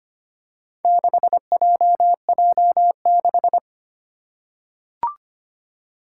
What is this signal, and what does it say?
Morse code, heard as a keyed tone, about 25 wpm, 700 Hz. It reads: 6JJ6
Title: Four Challenging Letter-Number Combinations 25wpm